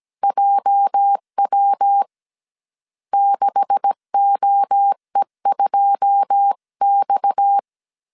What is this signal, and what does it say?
Morse code, heard as a keyed tone, about 17 wpm, 800 Hz. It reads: JW 6OE2X
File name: CW / DET / 1 9 1500C S C 16wpm 2909